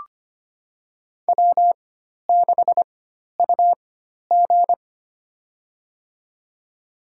Morse code, heard as a keyed tone, about 25 wpm, spaced as for 12 wpm, 700 Hz. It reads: W6UG
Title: USA Call Signs - 4x Character Spacing - 25wpm